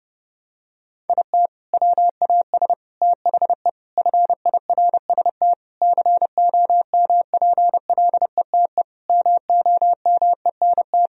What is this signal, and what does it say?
Morse code, heard as a keyed tone, about 32 wpm, 700 Hz. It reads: IT WAS THE FIRST COMPLETE MOMENT
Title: Sentences from Top 1000 Words - Review Part 2 - 30wpm